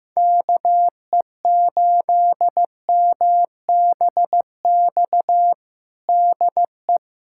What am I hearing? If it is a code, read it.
Morse code, heard as a keyed tone, about 15 wpm, 700 Hz. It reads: KE8MBX DE
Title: USA Callsign DE Callsign 15wpm